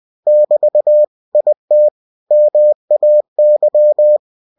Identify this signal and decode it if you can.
Morse code, heard as a keyed tone, about 20 wpm, 600 Hz. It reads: <BT> IT MAY
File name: winnie-20wpm